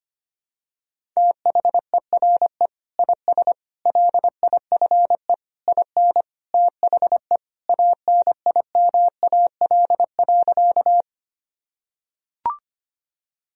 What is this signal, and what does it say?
Morse code, heard as a keyed tone, about 25 wpm, 700 Hz. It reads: THERE IS LIFE IN THE ANIMAL.
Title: Sentences from Top 1000 Words - Review Part 1 - 25wpm